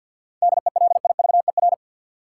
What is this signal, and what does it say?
Morse code, heard as a keyed tone, about 49 wpm, 700 Hz. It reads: DELIVER
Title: Seven Letter Words - 50wpm